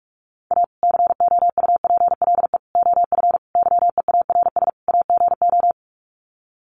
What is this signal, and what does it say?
Morse code, heard as a keyed tone, about 44 wpm, 700 Hz. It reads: A COUPLE OF YEARS AGO
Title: Sets of 5 Words 45wpm